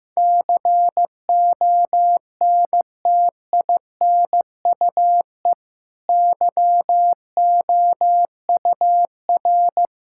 Morse code, heard as a keyed tone, about 15 wpm, 700 Hz. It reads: CONTINUE YOUR
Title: Sentences from Top 1000 Words - Review Part 6 - 15wpm